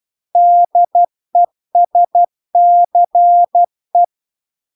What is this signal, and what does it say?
Morse code, heard as a keyed tone, about 12 wpm, 700 Hz. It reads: DESCE